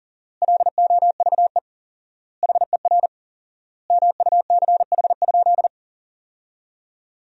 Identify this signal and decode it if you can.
Morse code, heard as a keyed tone, about 40 wpm, 700 Hz. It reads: LOVE HER MUCH?
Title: 4x Word-Spacing - Sentences from Top 400 Words - 40wpm